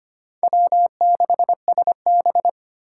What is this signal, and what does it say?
Morse code, heard as a keyed tone, about 25 wpm, 700 Hz. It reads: W6SB
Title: USA Call Signs - 4 Characters - 2x Character Spacing - 25wpm